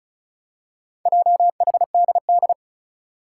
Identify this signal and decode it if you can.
Morse code, heard as a keyed tone, about 36 wpm, 700 Hz. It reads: JHDD
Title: Four Challenging Letter-Number Combinations 35wpm